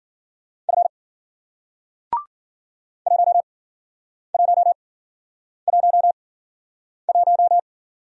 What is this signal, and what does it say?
Morse code, heard as a keyed tone, about 55 wpm, spaced as for 9 wpm, 700 Hz. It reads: F 1111